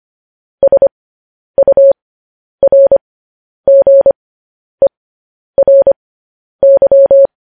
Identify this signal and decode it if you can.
Morse code, heard as a keyed tone, about 25 wpm, spaced as for 10 wpm, 550 Hz. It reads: SURGERY
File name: podcast / News headlines-1x-25x10